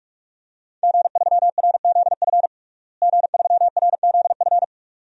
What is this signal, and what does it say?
Morse code, heard as a keyed tone, about 46 wpm, 700 Hz. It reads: G3RZF G3RZF
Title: DXCC-100 Call Signs - 5 Characters - Repeated 2x - 45wpm